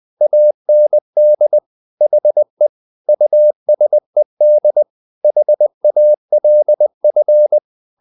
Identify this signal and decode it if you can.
Morse code, heard as a keyed tone, about 20 wpm, 600 Hz. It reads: AND HE USED HALF
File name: the old man and the sea part 2 20wpm